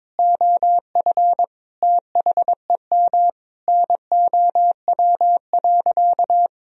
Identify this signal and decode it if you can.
Morse code, heard as a keyed tone, about 21 wpm, 700 Hz. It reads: OF THEM NOW.
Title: Sentences from Top 1000 Words - Review Part 1 - 22wpm